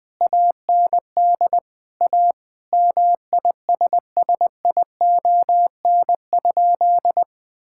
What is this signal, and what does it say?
Morse code, heard as a keyed tone, about 21 wpm, 700 Hz. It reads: AND A MISSION?
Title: Sentences from Top 1600 Words 20wpm